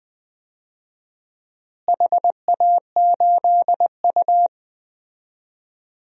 Morse code, heard as a keyed tone, about 20 wpm, 700 Hz. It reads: HA8U
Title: DXCC-100 Call Signs - 4 Characters - Repeated 3x - 20wpm